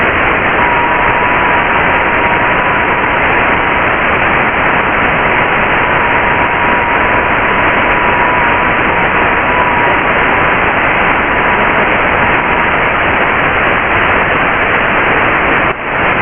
FT8.
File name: Radio / SDR Audio / audio z 19-48-00 11-09-2022